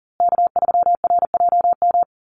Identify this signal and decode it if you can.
Morse code, heard as a keyed tone, about 40 wpm, 700 Hz. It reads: K3RJM
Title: USA Call Signs Repeated 2x 40wpm